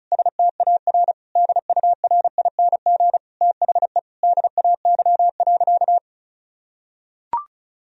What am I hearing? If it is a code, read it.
Morse code, heard as a keyed tone, about 36 wpm, 700 Hz. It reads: STAR DURING THE DAY.
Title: Sentences from Top 1000 Words - Review Part 2 - 35wpm